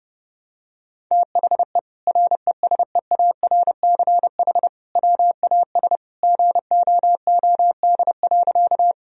Morse code, heard as a keyed tone, about 30 wpm, 700 Hz. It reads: THE RESEARCH WAS GOOD.
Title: Sentences from Top 1000 Words - Review Part 4 - 30wpm